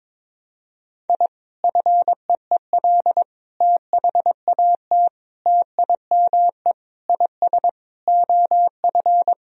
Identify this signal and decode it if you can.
Morse code, heard as a keyed tone, about 22 wpm, 700 Hz. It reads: I FEEL THAT TIME IS OF
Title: Sentences from Top 1000 Words - Review Part 3 - 22wpm